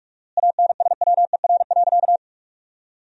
Morse code, heard as a keyed tone, about 47 wpm, 700 Hz. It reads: ANSWER.
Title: 4x Word-Spacing - Sentences from Top 500 Words - 45wpm